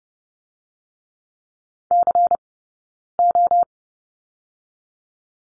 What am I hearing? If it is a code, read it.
Morse code, heard as a keyed tone, about 30 wpm, 700 Hz. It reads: C O
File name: 092 Lesson 16d - Letter c - Two Characters 30wpm